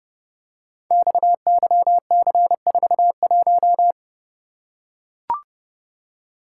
Morse code, heard as a keyed tone, about 32 wpm, 700 Hz. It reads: XYC41
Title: Code Groups - 5 Letter:Numbers - 30wpm